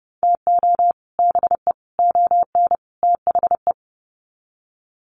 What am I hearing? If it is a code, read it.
Morse code, heard as a keyed tone, about 30 wpm, 700 Hz. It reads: TO BE ON THE
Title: Sets of 4 Words 30wpm